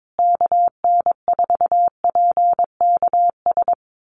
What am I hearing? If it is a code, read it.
Morse code, heard as a keyed tone, about 22 wpm, 700 Hz. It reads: KN4PKS
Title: USA Call Signs Repeated 2x 22wpm